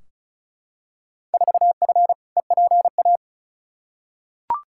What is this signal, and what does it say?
Morse code, heard as a keyed tone, about 34 wpm, 700 Hz. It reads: 4F EPA
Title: Field Day Exchanges Encore - 35wpm